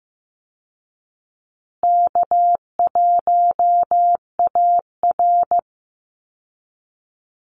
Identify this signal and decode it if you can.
Morse code, heard as a keyed tone, about 15 wpm, 700 Hz. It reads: K1AR
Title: Easy Call Signs Repeated 3x - 15wpm